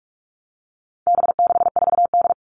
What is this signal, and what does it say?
Morse code, heard as a keyed tone, about 44 wpm, 700 Hz. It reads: B64D
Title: All Cut Numbers - 4-Digit 45wpm